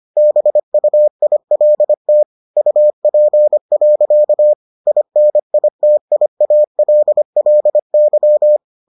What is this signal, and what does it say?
Morse code, heard as a keyed tone, about 25 wpm, 600 Hz. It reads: BUILT UP. INITIALLY